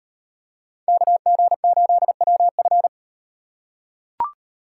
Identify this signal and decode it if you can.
Morse code, heard as a keyed tone, about 37 wpm, 700 Hz. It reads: KG8WF